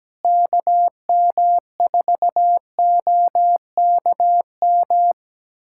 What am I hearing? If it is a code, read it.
Morse code, heard as a keyed tone, about 17 wpm, 700 Hz. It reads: KM4OKM
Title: USA Callsign DE Callsign 17wpm